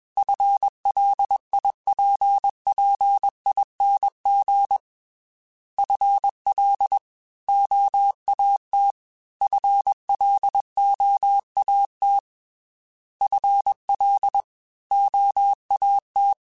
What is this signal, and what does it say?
Morse code, heard as a keyed tone, about 21 wpm, 800 Hz. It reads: FLIPPING FL OAT FLOAT FL OAT
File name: FL / FL mp3s / Fast FL @20